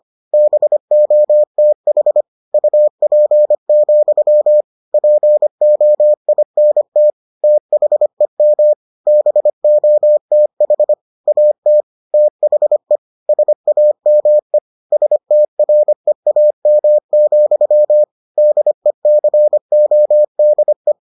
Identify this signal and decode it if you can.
Morse code, heard as a keyed tone, about 25 wpm, 600 Hz. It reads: BOTH UP, POINT THEM BOTH AT THE SAME STREAM, DECODE